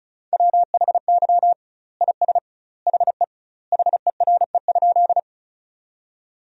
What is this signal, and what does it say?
Morse code, heard as a keyed tone, about 34 wpm, 700 Hz. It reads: WHY IS HE HERE?
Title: 2x Word-Spacing - Sentences from Top 200 Words - 35wpm